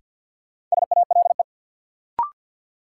Morse code, heard as a keyed tone, about 47 wpm, 700 Hz. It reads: SALE
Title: Sentences from Top 1400 Words - New Word Drill Alt - 50wpm